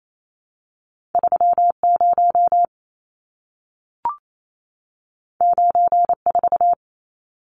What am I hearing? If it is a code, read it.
Morse code, heard as a keyed tone, about 28 wpm, 700 Hz. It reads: 30 94
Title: Numbers - 2-Digit 28wpm